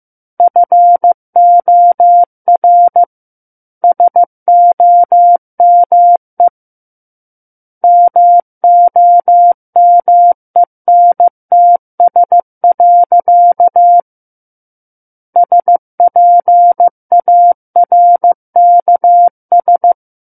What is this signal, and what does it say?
Morse code, heard as a keyed tone, about 15 wpm, 700 Hz. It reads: FOR SOME MOMENTS. SPARKS
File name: Book One / 06-Chapter 6 - 15 WPM